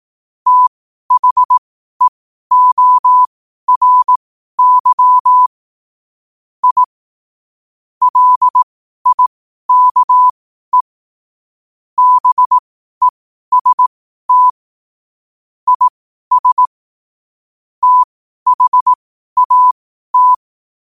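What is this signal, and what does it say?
Morse code, heard as a keyed tone, about 18 wpm, spaced as for 12 wpm, 1000 Hz. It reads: THEORY I LIKE BEST IS THAT
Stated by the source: Morse code sent at 12 words per minute.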